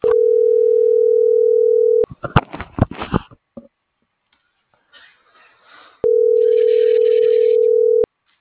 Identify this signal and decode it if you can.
Ringback tone (North America), 440 + 480 Hz.